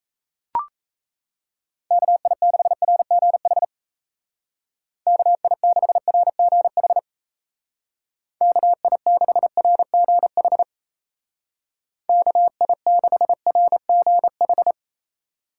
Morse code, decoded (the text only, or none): KI6RGH KI6RGH KI6RGH KI6RGH